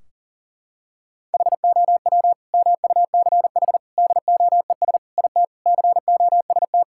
Morse code, heard as a keyed tone, about 40 wpm, 700 Hz. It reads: HOW MUCH DOES IT COST
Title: Sentences from Top 1600 Words 40wpm